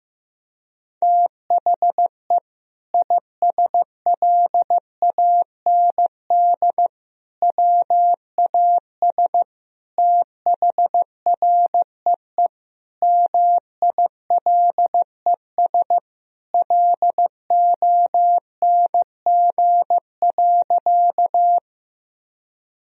Morse code, heard as a keyed tone, about 15 wpm, 700 Hz. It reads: THE ISLAND WAS THREE MILES LONG.
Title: Sentences from Top 1000 Words - Review Part 3 - 15wpm